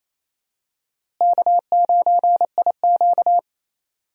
Morse code, heard as a keyed tone, about 28 wpm, 700 Hz. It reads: K9IQ
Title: USA Call Signs - 4 Characters - Repeated 2x - 28wpm